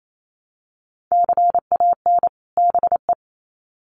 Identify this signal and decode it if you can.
Morse code, heard as a keyed tone, about 28 wpm, 700 Hz. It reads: CAN BE
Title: Sets of 2 Words 28wpm